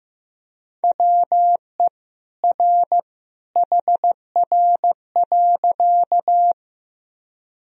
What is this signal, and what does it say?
Morse code, heard as a keyed tone, about 15 wpm, 700 Hz. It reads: WE R HR.
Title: Sentences from Top 200 Words and Abbreviations 15wpm